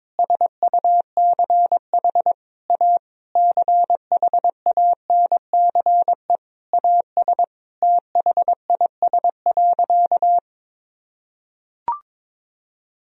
Morse code, heard as a keyed tone, about 22 wpm, 700 Hz. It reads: SUCH A CHANCE AS THIS.
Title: Sentences from Top 1000 Words - Review Part 1 - 22wpm